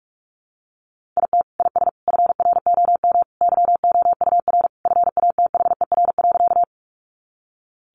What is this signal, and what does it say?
Morse code, heard as a keyed tone, about 44 wpm, 700 Hz. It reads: IT IS FROM YOUR FATHER.
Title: Sentences from Top 300 Words 45wpm